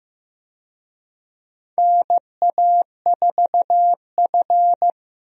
Morse code, heard as a keyed tone, about 15 wpm, 700 Hz. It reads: NA4F
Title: USA Call Signs - 4 Characters - Repeated 2x - 15wpm